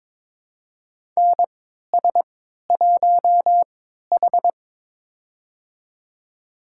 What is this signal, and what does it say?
Morse code, heard as a keyed tone, about 22 wpm, spaced as for 13 wpm, 700 Hz. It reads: NS1H
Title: USA Call Signs - 4 Characters - 3x Character Spacing - 22wpm